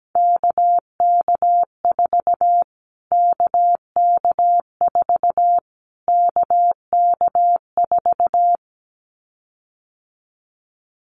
Morse code, heard as a keyed tone, about 17 wpm, 700 Hz. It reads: KK4 KK4 KK4
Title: Callsign Prefixes for Top 25 DXCC - Repeated 3x - 17wpm